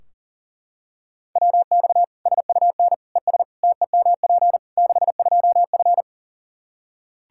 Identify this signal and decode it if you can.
Morse code, heard as a keyed tone, about 40 wpm, 700 Hz. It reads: WX SUN ES TEMP 62F
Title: Practice QSO - Exchange 2 - Info - 40wpm